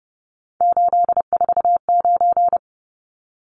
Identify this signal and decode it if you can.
Morse code, heard as a keyed tone, about 30 wpm, 700 Hz. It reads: 849
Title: Numbers - 3-Digit 30wpm